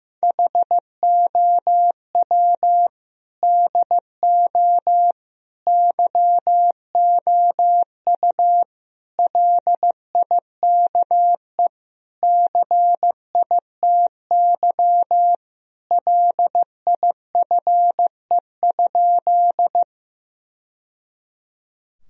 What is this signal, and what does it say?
Morse code, heard as a keyed tone, about 15 wpm, 700 Hz. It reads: HOW DO YOU LIKE CITY LIFE?
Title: Sentences from Top 1000 Words - Review Part 1 - 15wpm